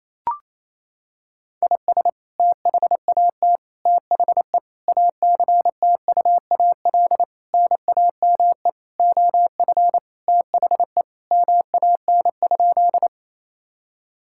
Morse code, heard as a keyed tone, about 28 wpm, 700 Hz. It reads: IS THAT THE ACTUAL NAME OF THE MAN?
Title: Sentences from Top 1000 Words - Review Part 5 - 28wpm